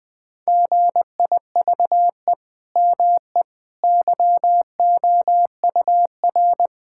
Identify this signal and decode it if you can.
Morse code, heard as a keyed tone, about 20 wpm, 700 Hz. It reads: GIVE ME YOUR
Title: Sentences from Top 1000 Words - Review Part 6 - 20wpm